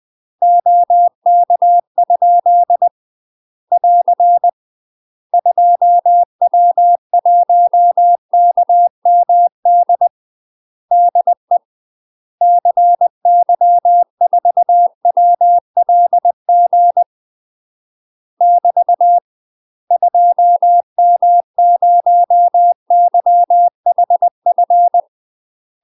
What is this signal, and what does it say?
Morse code, heard as a keyed tone, about 20 wpm, 700 Hz. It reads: OK? <AR> 2W1KMD DE CY4WLG <BT> 2M0YHF
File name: UK MASTER QSOs TRAINING/20WPM